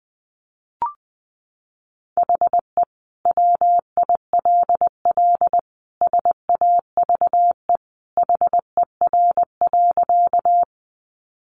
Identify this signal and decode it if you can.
Morse code, heard as a keyed tone, about 20 wpm, 700 Hz. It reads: HE WILL SAVE HER.
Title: Sentences from Top 1100 Words 20wpm